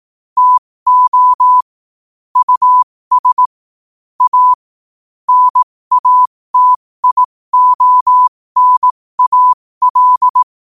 Morse code, heard as a keyed tone, about 18 wpm, spaced as for 15 wpm, 1000 Hz. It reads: TO US A NATIONAL
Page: New quotes every day in morse code at 15 Words per minute.